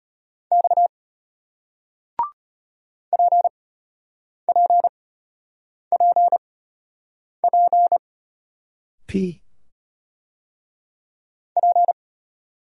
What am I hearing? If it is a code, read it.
Morse code, heard as a keyed tone, about 33 wpm, spaced as for 8 wpm, 700 Hz. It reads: X PPPP P